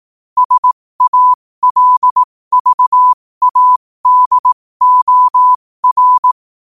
Morse code, heard as a keyed tone, about 18 wpm, spaced as for 15 wpm, 1000 Hz. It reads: SALVADOR
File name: qotdfiles / 15wpm